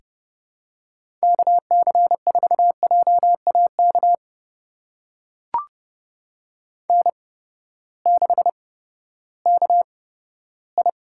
Morse code, heard as a keyed tone, about 30 wpm, spaced as for 15 wpm, 700 Hz. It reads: KC4JAK N6KI
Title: USA Call Signs - 8x Character Spacing - 30wpm